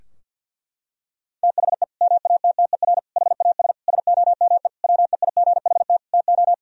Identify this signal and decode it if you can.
Morse code, heard as a keyed tone, about 51 wpm, 700 Hz. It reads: THE MATTER HAS SOME WEIGHT TO
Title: Sentences from Top 1000 Words - Review Part 3 - 50wpm